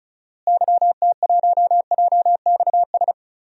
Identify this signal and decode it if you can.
Morse code, heard as a keyed tone, about 36 wpm, 700 Hz. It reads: YT1JXS
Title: DXCC-100 Call Signs - 6 Characters - Repeated 2x - 35wpm